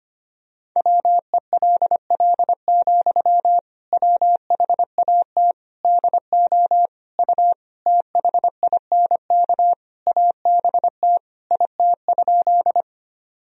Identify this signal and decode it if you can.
Morse code, heard as a keyed tone, about 25 wpm, 700 Hz. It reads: WELL, WHAT DO U THINK ABT IT?
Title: Sentences from Top 200 Words and Abbreviations 25wpm